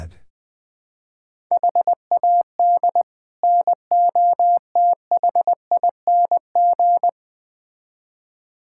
Morse code, heard as a keyed tone, about 20 wpm, 700 Hz. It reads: HAD NOTHING
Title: Build A Phrase - 4-Word Phrases - 20wpm